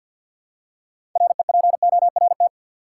Morse code, heard as a keyed tone, about 53 wpm, 700 Hz. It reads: REPORT
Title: Six Letter Words - 50wpm